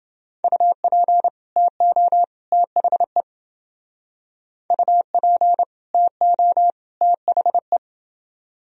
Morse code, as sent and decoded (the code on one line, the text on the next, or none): ..- .--. / - --- / - .... / ..- .--. / - --- / - .... .
UP TO THE UP TO THE